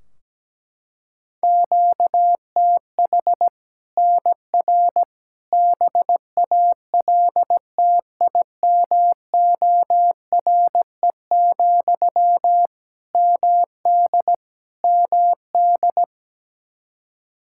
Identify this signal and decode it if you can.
Morse code, heard as a keyed tone, about 17 wpm, 700 Hz. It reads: QTH NR BALTIMORE, MD MD
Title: Practice QSO - Exchange 1 - Beginning - 17wpm